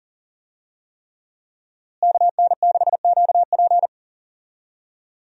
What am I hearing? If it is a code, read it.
Morse code, heard as a keyed tone, about 40 wpm, 700 Hz. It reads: KN6QP
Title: USA Call Signs - 5 Characters - Repeated 3x - 40wpm